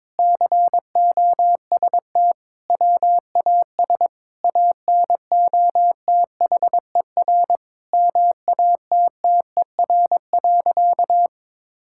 Morse code, heard as a keyed tone, about 22 wpm, 700 Hz. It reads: COST WAS ANOTHER MATTER.
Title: Sentences from Top 1000 Words - Review Part 5 - 22wpm